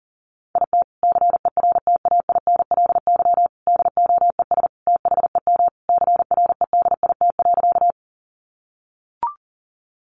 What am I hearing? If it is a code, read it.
Morse code, heard as a keyed tone, about 40 wpm, 700 Hz. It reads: IT CERTAINLY DOES THEM CREDIT.
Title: Sentences from Top 1000 Words 40wpm